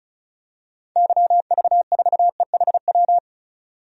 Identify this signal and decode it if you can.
Morse code, heard as a keyed tone, about 36 wpm, 700 Hz. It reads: YV4EHW
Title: DXCC-100 Call Signs - 6 Characters - 35wpm